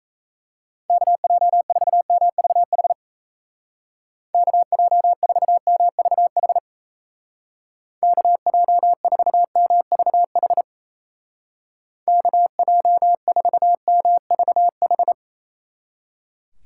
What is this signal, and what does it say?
Morse code, heard as a keyed tone, about 34 wpm, 700 Hz. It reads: KJ4MVH KJ4MVH KJ4MVH KJ4MVH
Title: Speed Racing - USA Call Signs 28wpm